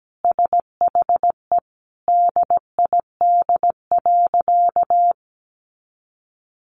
Morse code, heard as a keyed tone, about 17 wpm, 700 Hz. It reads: SHE DID.
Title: Sentences from Top 100 Words 17wpm